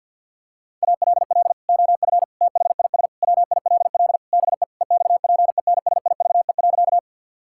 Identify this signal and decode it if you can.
Morse code, heard as a keyed tone, about 50 wpm, 700 Hz. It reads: ALL OF THIS WILL BE EXPENSIVE.
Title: Sentences from Top 1000 Words - Review Part 6 - 50wpm